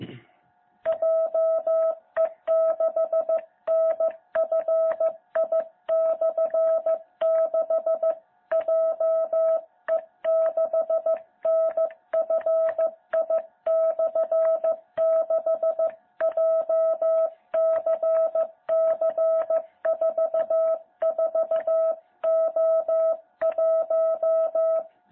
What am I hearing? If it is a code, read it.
Morse code, heard as a keyed tone, about 15 wpm, 660 Hz. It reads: JE6NFI/6 JE6NFI/6 JCC44O1
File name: jh8cbh / 2021cw